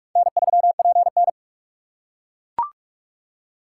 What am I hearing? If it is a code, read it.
Morse code, heard as a keyed tone, about 42 wpm, 700 Hz. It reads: N3PN
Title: USA Call Signs - 4 Characters - 6x Character Spacing - 45wpm